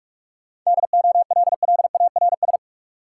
Morse code, heard as a keyed tone, about 42 wpm, 700 Hz. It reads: DOLLARS
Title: Sets of 5 Words - New Word Drill - 45wpm